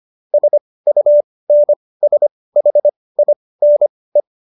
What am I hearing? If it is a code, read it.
Morse code, heard as a keyed tone, about 25 wpm, spaced as for 18 wpm, 590 Hz. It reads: SUNSHINE